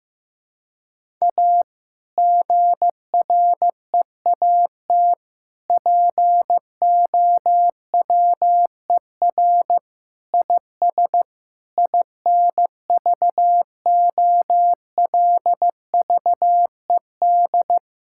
Morse code, heard as a keyed tone, about 15 wpm, 700 Hz. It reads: A GREAT POWER IS INVOLVED
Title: Sentences from Top 1000 Words - Review Part 5 - 15wpm